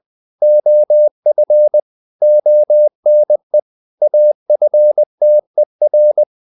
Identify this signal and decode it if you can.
Morse code, heard as a keyed tone, about 20 wpm, 600 Hz. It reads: OF ONE AFTER
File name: the old man and the sea part 2 20wpm